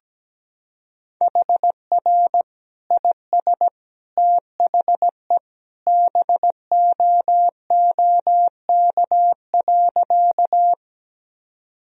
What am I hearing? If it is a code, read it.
Morse code, heard as a keyed tone, about 17 wpm, 700 Hz. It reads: HR IS THE BOOK.
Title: Sentences from Top 300 Words and Abbreviations 17wpm